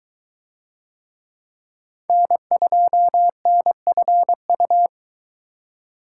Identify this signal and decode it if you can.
Morse code, heard as a keyed tone, about 22 wpm, 700 Hz. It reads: N2NFU